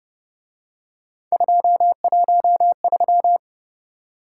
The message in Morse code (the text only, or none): 213